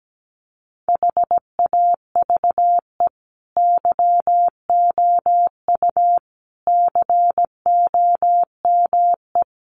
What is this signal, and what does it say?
Morse code, heard as a keyed tone, about 17 wpm, 700 Hz. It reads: HAVE YOU COME
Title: Sentences from Top 1400 Words 17wpm